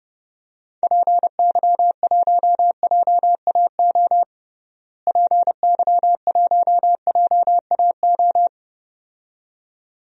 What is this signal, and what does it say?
Morse code, heard as a keyed tone, about 30 wpm, 700 Hz. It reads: PY1JAO PY1JAO
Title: DXCC-100 Call Signs - 6 Characters - Repeated 2x - 30wpm